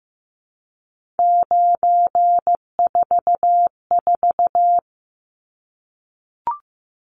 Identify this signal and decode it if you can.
Morse code, heard as a keyed tone, about 15 wpm, 700 Hz. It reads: 944
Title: Numbers - 3-Digit 15wpm